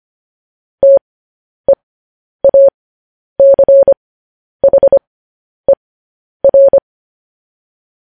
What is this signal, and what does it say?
Morse code, heard as a keyed tone, about 25 wpm, spaced as for 10 wpm, 550 Hz. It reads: TEACHER
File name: podcast / News headlines-1x-25x10